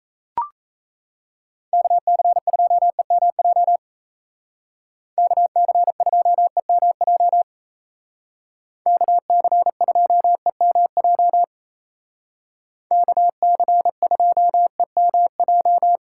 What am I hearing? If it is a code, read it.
Morse code, heard as a keyed tone, about 34 wpm, 700 Hz. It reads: KC2EMJ KC2EMJ KC2EMJ KC2EMJ
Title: Speed Racing - USA 2x3 Call Signs 28wpm